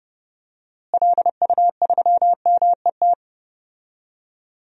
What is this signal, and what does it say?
Morse code, heard as a keyed tone, about 29 wpm, 700 Hz. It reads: LU3MET
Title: DXCC-100 Call Signs - 6 Characters - Repeated 3x - 30wpm